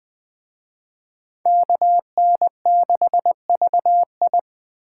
Morse code, heard as a keyed tone, about 24 wpm, 700 Hz. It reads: KN6VI
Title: Speed Racing - USA 2x2 Call Signs 20wpm